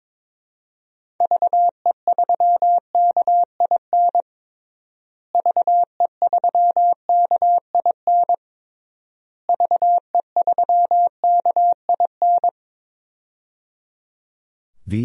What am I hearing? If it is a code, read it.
Morse code, heard as a keyed tone, about 22 wpm, 700 Hz. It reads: VE3KIN VE3KIN VE3KIN
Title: DXCC-100 Call Signs - 6 Characters - Repeated 3x - 22wpm